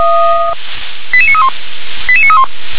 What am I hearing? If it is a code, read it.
Five-tone selective call (ZVEI), heard as quick runs of five tones, tones of about 70 ms. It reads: 88931 88931